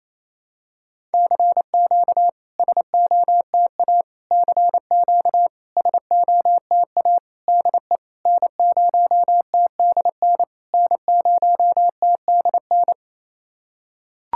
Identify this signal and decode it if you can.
Morse code, heard as a keyed tone, about 28 wpm, 690 Hz. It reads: CQ SOTA CQ SOTA DE N0TDN N0TDN
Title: SOTA QSO Practice 28wpm